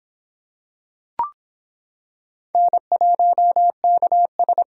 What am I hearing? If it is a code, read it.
Morse code, heard as a keyed tone, about 26 wpm, 700 Hz. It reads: N1KS